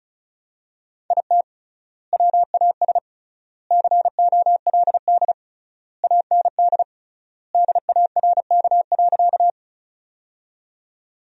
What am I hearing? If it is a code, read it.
Morse code, heard as a keyed tone, about 34 wpm, 700 Hz. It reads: IT WAS COLD AND DARK.
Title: 3x Word-Spacing - Sentences from Top 500 Words - 35wpm